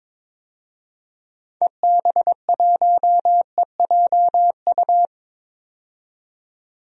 Morse code, heard as a keyed tone, about 22 wpm, 700 Hz. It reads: EB1EJU
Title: DXCC-100 Call Signs - 6 Characters - 22wpm